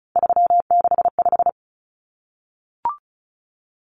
Morse code, heard as a keyed tone, about 34 wpm, 700 Hz. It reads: 365
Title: Common Cut Numbers - 3-Digit 35wpm